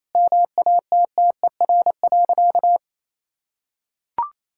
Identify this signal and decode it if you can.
Morse code, heard as a keyed tone, about 28 wpm, 700 Hz. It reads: MATTER.
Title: Sentences from Top 1600 Words 28wpm